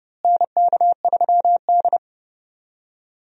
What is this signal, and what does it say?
Morse code, heard as a keyed tone, about 31 wpm, 700 Hz. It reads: NK3D
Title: USA Call Signs - 4 Characters - Repeated 2x - 30wpm